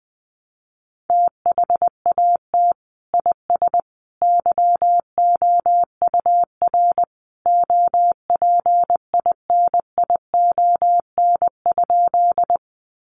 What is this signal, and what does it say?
Morse code, heard as a keyed tone, about 20 wpm, 700 Hz. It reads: THAT IS YOUR OPINION?
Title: Sentences from Top 1200 Words 20wpm